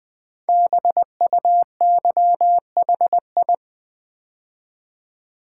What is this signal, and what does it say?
Morse code, heard as a keyed tone, about 20 wpm, 700 Hz. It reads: BUYHI
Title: Code Groups - 5 Letters - 20wpm